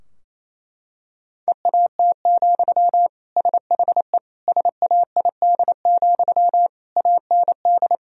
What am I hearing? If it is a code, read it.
Morse code, heard as a keyed tone, about 28 wpm, 700 Hz. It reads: EAT, SHE SAID, AND
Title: Sentences from Top 1000 Words - Review Part 2 - 28wpm